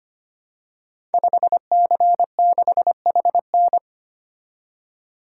Morse code, heard as a keyed tone, about 25 wpm, 700 Hz. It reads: 5C6HN
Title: Code Groups - 5 Letter:Numbers - 25wpm